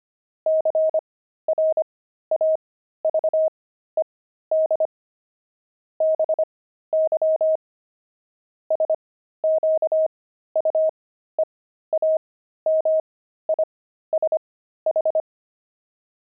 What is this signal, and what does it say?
Morse code, heard as a keyed tone, about 25 wpm, spaced as for 13 wpm, 630 Hz. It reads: CRAVED BY SQUEAMISH
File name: pangram-fs-13-cs-25-pigeons